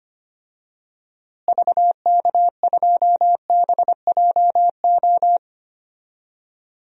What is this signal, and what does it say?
Morse code, heard as a keyed tone, about 25 wpm, 700 Hz. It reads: VK2BJO
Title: DXCC-100 Call Signs - 6 Characters - Repeated 2x - 25wpm